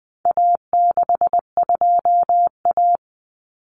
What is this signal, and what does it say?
Morse code, heard as a keyed tone, about 20 wpm, 700 Hz. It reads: A62A
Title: Common Cut Numbers - 4-Digit 20wpm